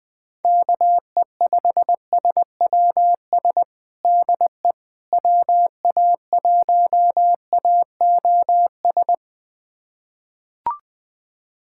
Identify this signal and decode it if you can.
Morse code, heard as a keyed tone, about 20 wpm, 700 Hz. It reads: KE5SWS DE WA1AOS
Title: USA Callsign DE Callsign - 6 Characters - 20wpm